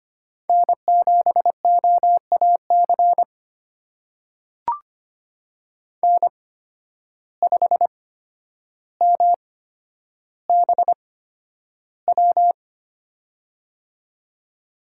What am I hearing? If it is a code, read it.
Morse code, heard as a keyed tone, about 25 wpm, spaced as for 11 wpm, 700 Hz. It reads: N7OAC N5MBW
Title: USA Call Signs - 5 Characters - 8x Character Spacing - 25wpm